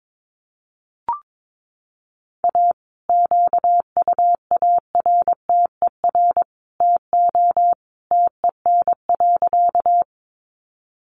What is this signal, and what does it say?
Morse code, heard as a keyed tone, about 22 wpm, 700 Hz. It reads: A QUARTER TO TEN.
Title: Sentences from Top 1300 Words 22wpm